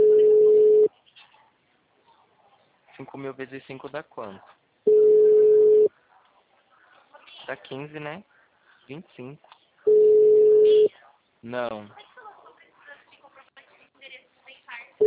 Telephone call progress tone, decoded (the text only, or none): ringback tone